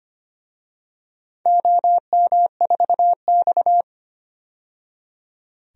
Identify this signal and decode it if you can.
Morse code, heard as a keyed tone, about 25 wpm, 700 Hz. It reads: OM4X
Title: DXCC-100 Call Signs - 4 Characters - 25wpm